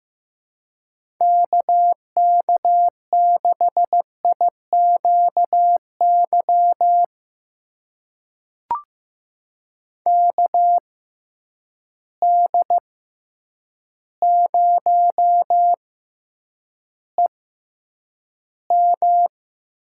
Morse code, heard as a keyed tone, about 15 wpm, spaced as for 8 wpm, 700 Hz. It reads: KK6IQY KD0EM
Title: USA Call Signs - 6 Characters - 6x Character Spacing - 15wpm